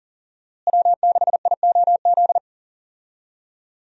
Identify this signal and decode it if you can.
Morse code, heard as a keyed tone, about 40 wpm, 700 Hz. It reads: W6IOZ
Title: USA Call Signs - 5 Characters - Repeated 2x - 40wpm